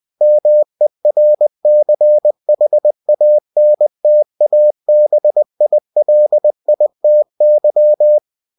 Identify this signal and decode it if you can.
Morse code, heard as a keyed tone, about 20 wpm, 600 Hz. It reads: MERCHANTABILITY
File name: cw-books / winnie-the-pooh / winnie-20wpm-page-0119